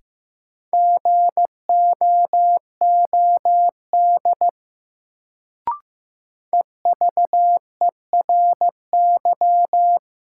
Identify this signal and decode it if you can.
Morse code, heard as a keyed tone, about 16 wpm, 700 Hz. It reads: GOOD EVERY
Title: Sentences from Top 200 Words - New Word Drill - 15wpm